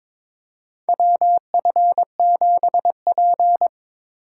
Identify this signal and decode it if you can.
Morse code, heard as a keyed tone, about 22 wpm, 700 Hz. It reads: WF7P
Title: USA Call Signs - 4 Characters - 6x Character Spacing - 22wpm